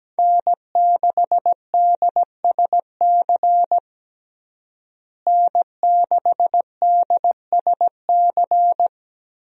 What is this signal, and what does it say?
Morse code, heard as a keyed tone, about 17 wpm, 700 Hz. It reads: N6DSC N6DSC
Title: USA Call Signs - 5 Characters - Repeated 2x - 17wpm